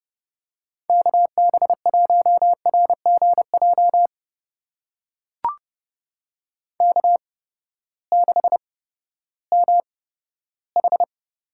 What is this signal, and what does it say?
Morse code, heard as a keyed tone, about 30 wpm, spaced as for 15 wpm, 700 Hz. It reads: KB1RGJ K6MH
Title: USA Call Signs - 8x Character Spacing - 30wpm